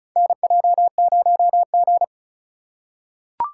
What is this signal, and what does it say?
Morse code, heard as a keyed tone, about 34 wpm, 700 Hz. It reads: NJ0G
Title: USA Call Signs - 4 Characters - 3x Character Spacing - 35wpm